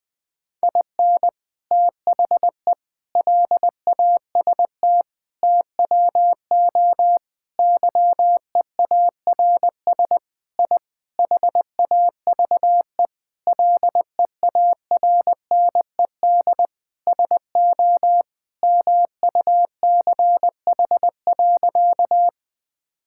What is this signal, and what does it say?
Morse code, heard as a keyed tone, about 20 wpm, 700 Hz. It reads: IN THE LAST TWO YEARS I HAVE LEARNED SO MUCH.
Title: Sentences with 5-Word Phrases - Spoken Beginning - 20wpm